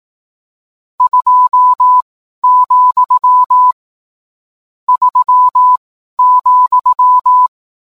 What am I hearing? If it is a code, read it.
Morse code, heard as a keyed tone, about 18 wpm, spaced as for 12 wpm, 1000 Hz. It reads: 2, 3,